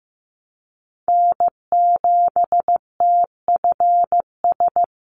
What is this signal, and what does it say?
Morse code, heard as a keyed tone, about 15 wpm, 700 Hz. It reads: N7TFS
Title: USA Call Signs 15wpm